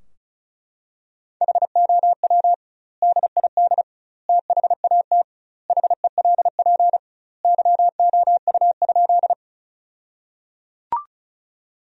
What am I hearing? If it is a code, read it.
Morse code, heard as a keyed tone, about 36 wpm, 700 Hz. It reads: HOW DID THAT HELP YOU?
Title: 2x Word-Spacing - Sentences from Top 200 Words - 35wpm